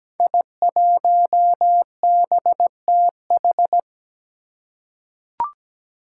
Morse code, heard as a keyed tone, about 17 wpm, 700 Hz. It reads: I1BTH
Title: Five Challenging Letter-Number Combinations 17wpm